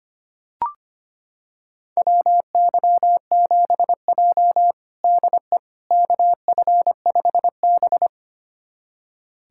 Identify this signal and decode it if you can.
Morse code, heard as a keyed tone, about 25 wpm, 700 Hz. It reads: WY7J DE KF5B
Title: USA Callsign DE Callsign - 4 Characters - 25wpm